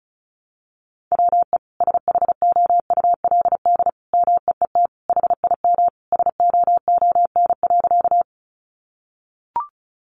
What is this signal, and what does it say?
Morse code, heard as a keyed tone, about 34 wpm, 700 Hz. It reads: WE SHOULD MEET HIM SOON.
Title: Sentences from Top 1100 Words 35wpm